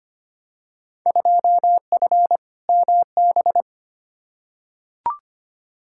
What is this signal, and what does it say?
Morse code, heard as a keyed tone, about 25 wpm, 700 Hz. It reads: 2F MB
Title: Field Day Exchanges Encore - 25wpm